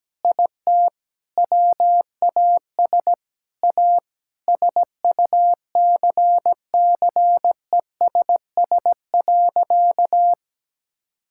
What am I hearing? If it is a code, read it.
Morse code, heard as a keyed tone, about 17 wpm, 700 Hz. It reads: IT WAS A SUCCESS.
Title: Sentences from Top 1000 Words - Review Part 6 - 17wpm